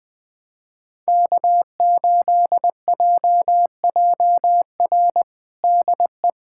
Morse code, heard as a keyed tone, about 20 wpm, 700 Hz. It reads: K8JJR DE
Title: USA Callsign DE Callsign - 5 Characters - 20wpm